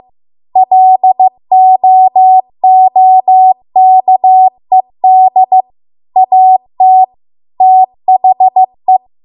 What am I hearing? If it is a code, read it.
Morse code, heard as a keyed tone, about 15 wpm, 750 Hz. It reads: LOOKED AT THE